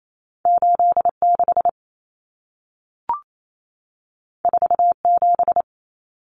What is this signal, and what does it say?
Morse code, heard as a keyed tone, about 28 wpm, 700 Hz. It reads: 86 47
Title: Numbers - 2-Digit 28wpm